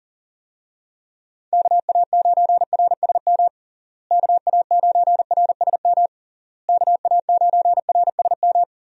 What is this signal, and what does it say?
Morse code, heard as a keyed tone, about 40 wpm, 700 Hz. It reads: KA9RSM KA9RSM KA9RSM
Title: USA Call Signs - 6 Characters - Repeated 3x - 40wpm